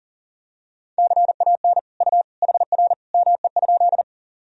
Morse code, heard as a keyed tone, about 37 wpm, 700 Hz. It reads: CAN U HR ME?
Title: Sentences from Top 300 Words and Abbreviations 40wpm